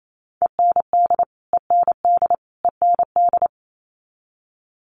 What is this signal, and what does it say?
Morse code, heard as a keyed tone, about 28 wpm, 700 Hz. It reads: END END END
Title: Top 500 Words Repeated 28wpm